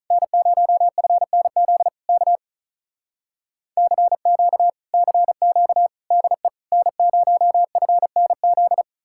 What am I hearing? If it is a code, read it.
Morse code, heard as a keyed tone, about 46 wpm, 700 Hz. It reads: N0FNZ K CQ CQ DE N0FNZ